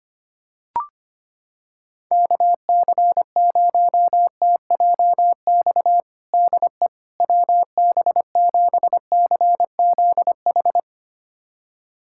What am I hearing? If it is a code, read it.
Morse code, heard as a keyed tone, about 25 wpm, 700 Hz. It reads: KC0TJX DE WB7CZH
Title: USA Callsign DE Callsign 25wpm